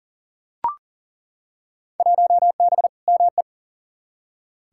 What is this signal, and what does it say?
Morse code, heard as a keyed tone, about 40 wpm, 700 Hz. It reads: 1B ME
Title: Field Day Exchanges - 40wpm